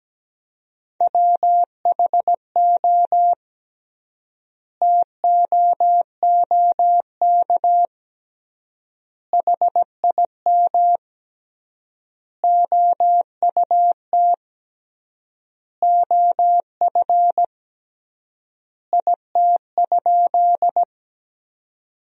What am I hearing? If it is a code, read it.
Morse code, heard as a keyed tone, about 17 wpm, 700 Hz. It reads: WHO TOOK HIM OUT OF IT?
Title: 3x Word-Spacing - Sentences from Top 300 Words - 17wpm